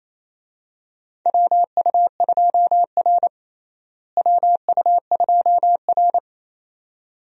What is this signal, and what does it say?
Morse code, heard as a keyed tone, about 28 wpm, 700 Hz. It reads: WU2R WU2R
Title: USA Call Signs - 4 Characters - Repeated 2x - 28wpm